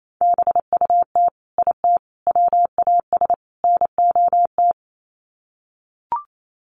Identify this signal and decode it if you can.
Morse code, heard as a keyed tone, about 28 wpm, 700 Hz. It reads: BUT IT WAS NOT
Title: Sets of 4 Words 28wpm